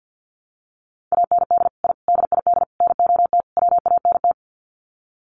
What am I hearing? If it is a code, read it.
Morse code, heard as a keyed tone, about 50 wpm, 700 Hz. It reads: AND I DID NOT WANT
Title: Sets of 5 Words 50wpm